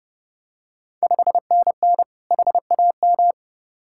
Morse code, heard as a keyed tone, about 32 wpm, 700 Hz. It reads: 5NN HAM
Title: Ontario QSO Party Exchange 30wpm